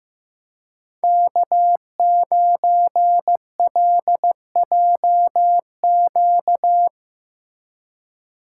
Morse code, heard as a keyed tone, about 15 wpm, 700 Hz. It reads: K9LJQ
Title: USA Call Signs - 6x Character Spacing - 15wpm